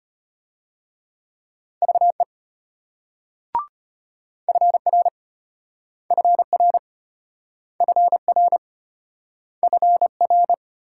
Morse code, heard as a keyed tone, about 33 wpm, 700 Hz. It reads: VE FR FR FR FR